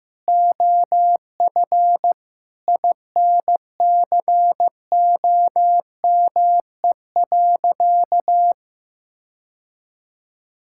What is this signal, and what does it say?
Morse code, heard as a keyed tone, about 15 wpm, 700 Hz. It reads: OF INCOME.
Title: Sentences from Top 1000 Words - Review Part 5 - 15wpm